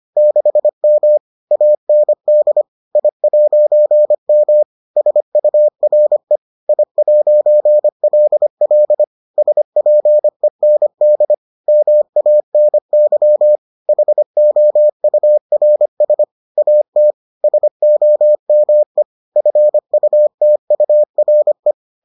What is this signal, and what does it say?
Morse code, heard as a keyed tone, about 25 wpm, 600 Hz. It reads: 6M AND I'M SURE I'LL SPEND MANY HOURS AT SOME FUTURE